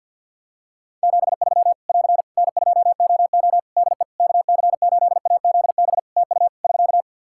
Morse code, heard as a keyed tone, about 50 wpm, 700 Hz. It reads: 73 <AR> N2OO DE KC8AZB TU <SK>
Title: Practice QSO - Exchange 4 - Ending - 50wpm